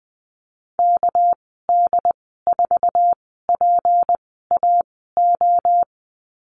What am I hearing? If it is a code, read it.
Morse code, heard as a keyed tone, about 20 wpm, spaced as for 15 wpm, 700 Hz. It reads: KD4PAO
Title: 200 Lesson 32g - Letter k - Call Signs 20wpm